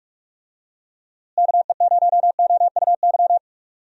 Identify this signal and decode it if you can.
Morse code, heard as a keyed tone, about 42 wpm, 700 Hz. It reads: KE0OUY
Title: USA Call Signs - 6 Characters - Repeated 3x - 45wpm